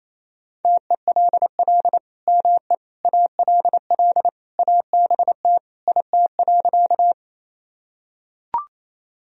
Morse code, heard as a keyed tone, about 28 wpm, 700 Hz. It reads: TELL ME ALL ABT IT.
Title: Sentences from Top 200 Words and Abbreviations 28wpm